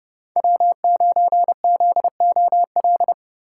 Morse code, heard as a keyed tone, about 30 wpm, 700 Hz. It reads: W9ZOL
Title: USA Call Signs - 5 Characters - 4x Character Spacing - 30wpm